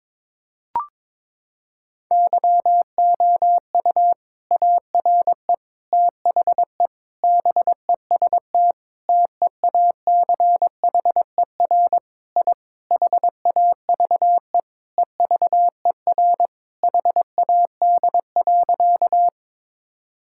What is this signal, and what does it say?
Morse code, heard as a keyed tone, about 21 wpm, 700 Hz. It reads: YOU ARE THE BEST TEACHER I HAVE EVER HAD.
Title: Sentences from Top 1000 Words - Review Part 6 - 22wpm